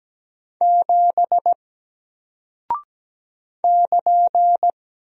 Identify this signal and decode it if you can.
Morse code, heard as a keyed tone, about 17 wpm, 700 Hz. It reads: 7 <KN>
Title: Single Character (Letter:Number:Punctuation:Prosign) Alt 17wpm